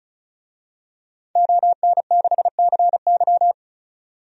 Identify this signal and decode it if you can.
Morse code, heard as a keyed tone, about 34 wpm, 700 Hz. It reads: ON6CY
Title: DXCC-100 Call Signs - 5 Characters - 35wpm